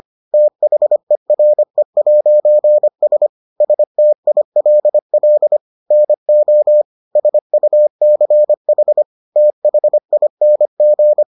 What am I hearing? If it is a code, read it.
Morse code, heard as a keyed tone, about 25 wpm, 600 Hz. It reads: THERE'S STILL NO SUCH THING